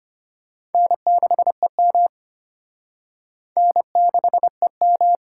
Morse code, heard as a keyed tone, about 29 wpm, 700 Hz. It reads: N6EM N6EM
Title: Speed Racing - USA 1x2 Call Signs 25wpm